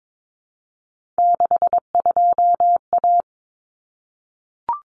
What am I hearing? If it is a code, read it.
Morse code, heard as a keyed tone, about 22 wpm, 700 Hz. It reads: 62A
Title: Common Cut Numbers - 3-Digit 22wpm